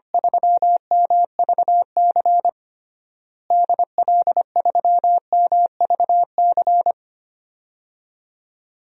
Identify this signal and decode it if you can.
Morse code, heard as a keyed tone, about 25 wpm, 700 Hz. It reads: 3MVC DL3MVC
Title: DXCC-100 Call Signs - 6 Characters - Repeated 3x - 25wpm